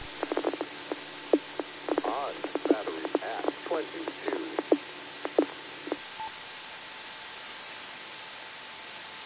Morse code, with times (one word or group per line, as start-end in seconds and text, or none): none